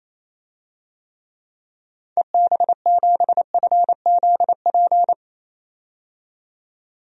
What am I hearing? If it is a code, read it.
Morse code, heard as a keyed tone, about 28 wpm, 700 Hz. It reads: EB7FZP
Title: DXCC-100 Call Signs - 6 Characters - 28wpm